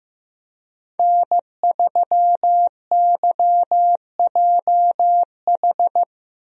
Morse code, heard as a keyed tone, about 15 wpm, 700 Hz. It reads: N3YJH
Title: USA Call Signs - 5 Characters - Repeated 3x - 15wpm